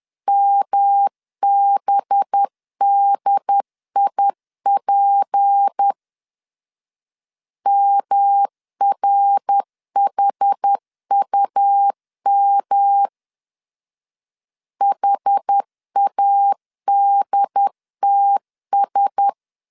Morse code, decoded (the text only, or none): MBDIP MRHUM HADTS